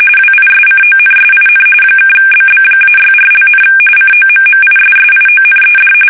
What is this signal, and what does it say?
RTTY, about 45.45 baud. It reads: WISCONSIN (BB-64), COMMISSIONED
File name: fsk850 / battleships